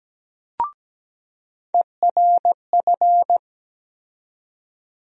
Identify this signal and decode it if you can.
Morse code, heard as a keyed tone, about 17 wpm, 700 Hz. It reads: ERF
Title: USA Call Sign Postfixes - 3 Letters 17wpm